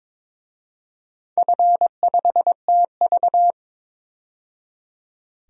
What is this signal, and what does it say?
Morse code, heard as a keyed tone, about 22 wpm, 700 Hz. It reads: F5TV
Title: DXCC-100 Call Signs - 4 Characters - 22wpm